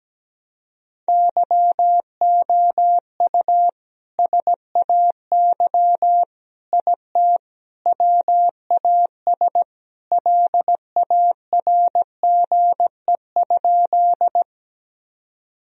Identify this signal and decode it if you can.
Morse code, heard as a keyed tone, about 17 wpm, 700 Hz. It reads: YOU SAY IT WAS LARGE?
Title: Sentences from Top 1000 Words - Review Part 1 - 17wpm